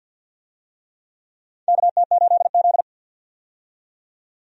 Morse code, heard as a keyed tone, about 53 wpm, 700 Hz. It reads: KT8Z
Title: USA Call Signs - 4 Characters - 50wpm